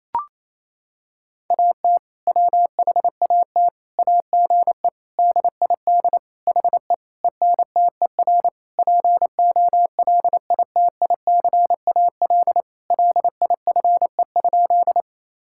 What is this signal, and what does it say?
Morse code, heard as a keyed tone, about 27 wpm, 700 Hz. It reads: AT WHAT AGE DID HE ENTER POLITICAL LIFE?
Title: Sentences from Top 1000 Words - Review Part 5 - 28wpm